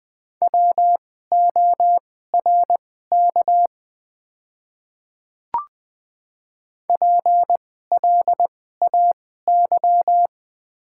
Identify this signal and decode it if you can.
Morse code, heard as a keyed tone, about 20 wpm, spaced as for 14 wpm, 700 Hz. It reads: WORK PLAY
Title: 2x Character Spacing - Top 200 Words - 20wpm